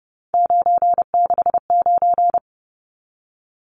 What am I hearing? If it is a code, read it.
Morse code, heard as a keyed tone, about 30 wpm, 700 Hz. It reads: 969
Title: Numbers - 3-Digit 30wpm